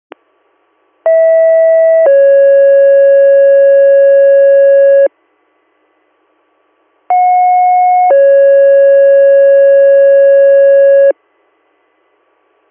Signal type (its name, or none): two-tone page